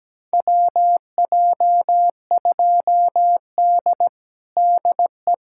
Morse code, heard as a keyed tone, about 17 wpm, 700 Hz. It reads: WJ2D DE
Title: USA Callsign DE Callsign - 4 Characters - 17wpm